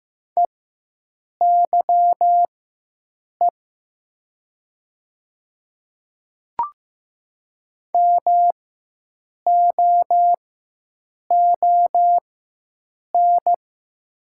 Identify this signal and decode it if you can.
Morse code, heard as a keyed tone, about 15 wpm, spaced as for 7 wpm, 700 Hz. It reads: EYE MOON
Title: 4x Character Spacing - Top 500 Words - 15wpm